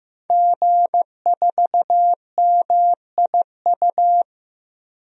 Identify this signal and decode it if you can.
Morse code, heard as a keyed tone, about 15 wpm, 700 Hz. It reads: G4MIU
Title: DXCC-100 Call Signs - 5 Characters - Repeated 2x - 15wpm